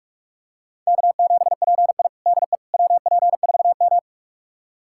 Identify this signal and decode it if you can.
Morse code, heard as a keyed tone, about 46 wpm, 700 Hz. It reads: K7PI DE WP4M
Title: USA Callsign DE Callsign - 4 Characters - 45wpm